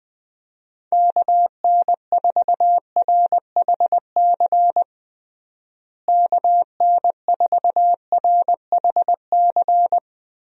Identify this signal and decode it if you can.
Morse code, heard as a keyed tone, about 20 wpm, 700 Hz. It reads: KN4RHC KN4RHC
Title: USA Call Signs - 6 Characters - Repeated 3x - 20wpm